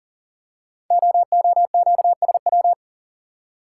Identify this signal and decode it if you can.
Morse code, heard as a keyed tone, about 39 wpm, 700 Hz. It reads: OOQSW
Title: Code Groups - 5 Letters - 40wpm